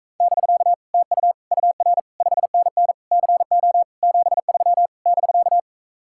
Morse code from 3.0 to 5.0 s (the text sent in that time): CO 73